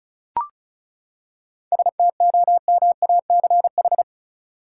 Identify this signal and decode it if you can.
Morse code, heard as a keyed tone, about 34 wpm, 700 Hz. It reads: STOMACH
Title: Seven Letter Words - 35wpm